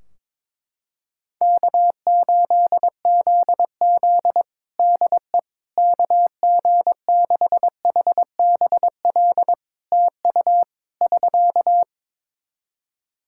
Morse code, heard as a keyed tone, about 22 wpm, 700 Hz. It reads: K8ZZ DE KG6HBL TU <SK>
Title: Practice QSO - Exchange 4 - Ending - 22wpm